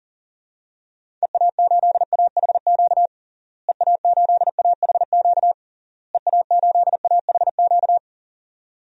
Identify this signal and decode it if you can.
Morse code, heard as a keyed tone, about 40 wpm, 700 Hz. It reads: EA8AHQ EA8AHQ EA8AHQ
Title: DXCC-100 Call Signs - 6 Characters - Repeated 3x - 40wpm